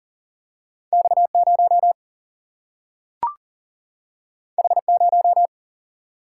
Morse code, heard as a keyed tone, about 40 wpm, 700 Hz. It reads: X0 H0
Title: Single-Letter Single-Number 40wpm